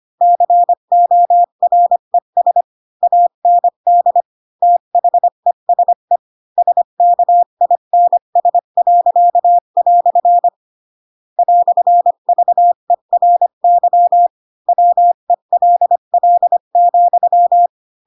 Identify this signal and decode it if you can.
Morse code, heard as a keyed tone, about 25 wpm, 700 Hz. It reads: CORES AND THESE SKINS." "VERY WELL,
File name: Pinocchio / Pinocchio ch7 20 25 ik1hge